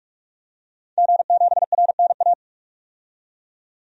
Morse code, heard as a keyed tone, about 42 wpm, 700 Hz. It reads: G7RNA
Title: DXCC-100 Call Signs - 5 Characters - 45wpm